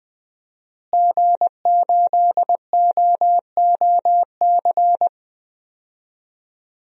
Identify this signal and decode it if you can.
Morse code, heard as a keyed tone, about 20 wpm, 700 Hz. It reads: G8OOC
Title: DXCC-100 Call Signs - 5 Characters - Repeated 3x - 20wpm